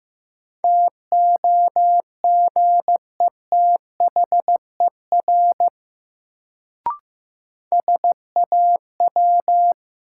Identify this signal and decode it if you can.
Morse code, heard as a keyed tone, about 15 wpm, 700 Hz. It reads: TOGETHER SAW
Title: Sentences from Top 300 Words - New Word Drill - 15wpm